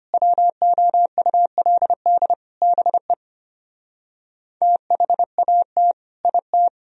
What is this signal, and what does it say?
Morse code, heard as a keyed tone, about 33 wpm, 690 Hz. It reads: WOULD BE THAT IT